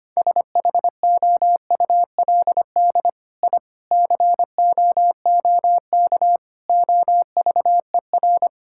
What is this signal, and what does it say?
Morse code, heard as a keyed tone, about 25 wpm, 700 Hz. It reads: SHOULD I COOK OVER
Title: Sentences from Top 1000 Words - Review Part 6 - 25wpm